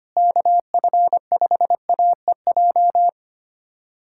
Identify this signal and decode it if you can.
Morse code, heard as a keyed tone, about 25 wpm, 700 Hz. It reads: KF5AEJ
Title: USA Callsign DE Callsign - 6 Characters - 25wpm